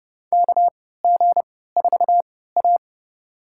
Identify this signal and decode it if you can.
Morse code, heard as a keyed tone, about 30 wpm, spaced as for 17 wpm, 700 Hz. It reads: KG4A
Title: USA Call Signs - 4 Characters - 3x Character Spacing - 30wpm